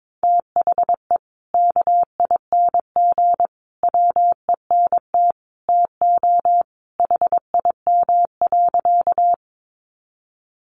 Morse code, heard as a keyed tone, about 22 wpm, 700 Hz. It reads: THE KING WENT TO HIM.
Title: Sentences from Top 400 Words 22wpm